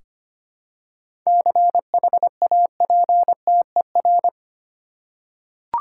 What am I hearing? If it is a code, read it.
Morse code, heard as a keyed tone, about 25 wpm, 700 Hz. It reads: CHAPTER
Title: Seven Letter Words - 25wpm